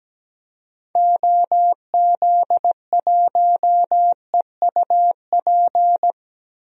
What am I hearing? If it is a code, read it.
Morse code, heard as a keyed tone, about 17 wpm, 700 Hz. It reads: OZ1EUP
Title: DXCC-100 Call Signs - 6 Characters - Repeated 2x - 17wpm